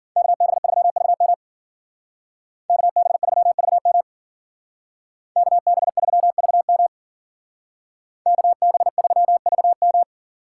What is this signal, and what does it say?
Morse code, heard as a keyed tone, about 50 wpm, 700 Hz. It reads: KB3VM KB3VM KB3VM KB3VM